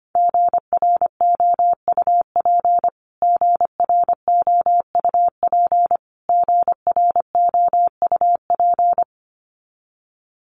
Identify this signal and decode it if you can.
Morse code, heard as a keyed tone, about 25 wpm, 700 Hz. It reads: GROUP GROUP GROUP
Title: Top 300 Words Repeated 25wpm